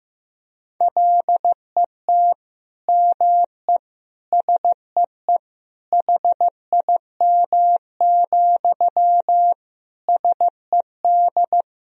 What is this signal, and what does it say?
Morse code, heard as a keyed tone, about 15 wpm, 700 Hz. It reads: LET ME SEE HIM, SED
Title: Sentences from Top 400 Words and Abbreviations 15wpm